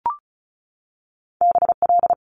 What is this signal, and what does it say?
Morse code, heard as a keyed tone, about 36 wpm, 700 Hz. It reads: BL
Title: Multi-letter Phonemes 35wpm